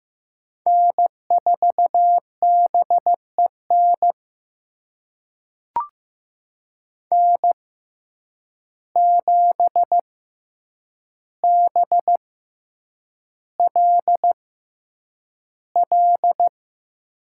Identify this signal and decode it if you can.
Morse code, heard as a keyed tone, about 15 wpm, spaced as for 8 wpm, 700 Hz. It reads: N4BEN N7BLL
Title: USA Call Signs - 5 Characters - 6x Character Spacing - 15wpm